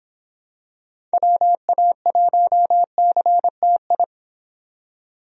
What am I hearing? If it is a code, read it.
Morse code, heard as a keyed tone, about 24 wpm, 700 Hz. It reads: WA1CTI